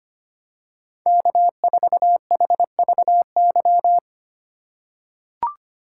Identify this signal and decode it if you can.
Morse code, heard as a keyed tone, about 25 wpm, 700 Hz. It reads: K4HVY
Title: USA Call Signs - 5 Characters - 4x Character Spacing - 25wpm